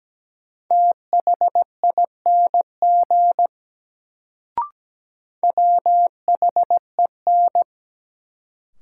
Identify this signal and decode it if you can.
Morse code, heard as a keyed tone, about 17 wpm, 700 Hz. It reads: THING WHEN
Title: Sentences from Top 100 Words - New Word Drill - 17wpm